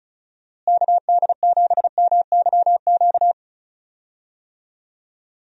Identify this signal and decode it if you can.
Morse code, heard as a keyed tone, about 36 wpm, 700 Hz. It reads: KD7MYQ
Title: USA Call Signs - 6 Characters - Repeated 3x - 35wpm